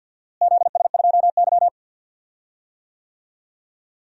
Morse code, heard as a keyed tone, about 48 wpm, 700 Hz. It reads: ZS2Y
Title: DXCC-100 Call Signs - 4 Characters - Repeated 2x - 50wpm